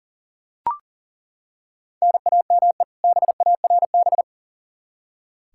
Morse code, heard as a keyed tone, about 37 wpm, 700 Hz. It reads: NAME BARB
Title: K1USN SST QSO Practice 40wpm